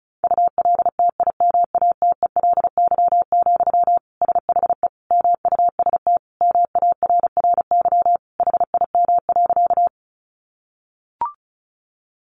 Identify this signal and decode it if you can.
Morse code, heard as a keyed tone, about 34 wpm, 700 Hz. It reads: ULTIMATELY, SHE MUST MARRY HIM.
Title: Sentences from Top 1200 Words 35wpm